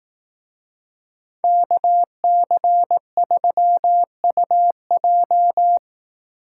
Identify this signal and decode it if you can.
Morse code, heard as a keyed tone, about 18 wpm, 700 Hz. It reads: KC3UJ
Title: Speed Racing - USA 2x2 Call Signs 15wpm